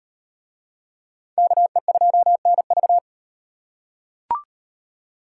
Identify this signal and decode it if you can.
Morse code, heard as a keyed tone, about 35 wpm, 700 Hz. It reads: KE2NV